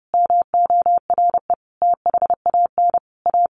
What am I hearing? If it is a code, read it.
Morse code, heard as a keyed tone, about 28 wpm, 700 Hz. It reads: MORE THAN A
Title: Sets of 5 Words Encore2 30wpm